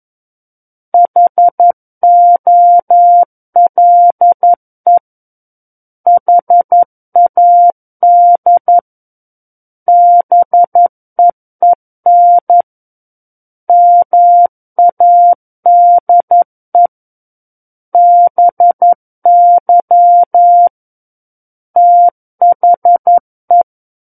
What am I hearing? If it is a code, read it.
Morse code, heard as a keyed tone, about 11 wpm, 700 Hz. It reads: HOLE HAD BEEN MADE BY THE
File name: Book One / 02-Chapter 2 - 11 WPM